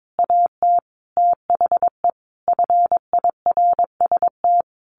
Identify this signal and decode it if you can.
Morse code, heard as a keyed tone, about 22 wpm, 700 Hz. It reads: AT THE FIRST
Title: A Sound Of Thunder 22wpm